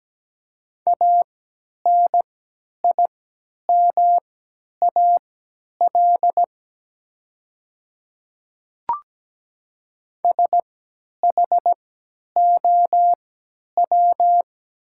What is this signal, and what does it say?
Morse code, heard as a keyed tone, about 17 wpm, spaced as for 10 wpm, 700 Hz. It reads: ANIMAL SHOW
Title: 3x Character Spacing - Top 200 Words - 17wpm